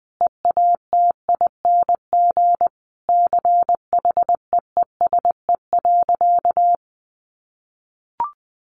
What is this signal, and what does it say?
Morse code, heard as a keyed tone, about 20 wpm, 700 Hz. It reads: EATING CHEESE.
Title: Easy Phrases 20wpm